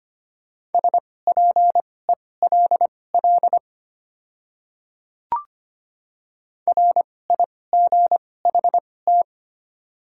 Morse code, heard as a keyed tone, about 25 wpm, spaced as for 18 wpm, 700 Hz. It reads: SPELL RIGHT
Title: 2x Character Spacing - Top 300 Words - 25wpm